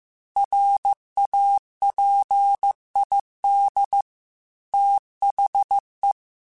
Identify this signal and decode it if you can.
Morse code, heard as a keyed tone, about 15 wpm, 800 Hz. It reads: RAPID THE
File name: Technical Subjects / Technical Subjects-mp3s / XL @13